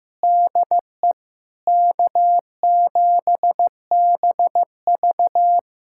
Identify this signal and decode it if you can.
Morse code, heard as a keyed tone, about 16 wpm, 700 Hz. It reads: DE K7BV
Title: Practice QSO - Exchange 4 - Ending - 15wpm